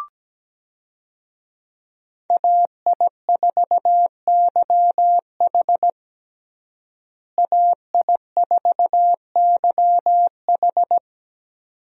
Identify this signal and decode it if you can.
Morse code, heard as a keyed tone, about 17 wpm, 700 Hz. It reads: AI4YH AI4YH
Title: USA Call Signs - 5 Characters - Repeated 2x - 17wpm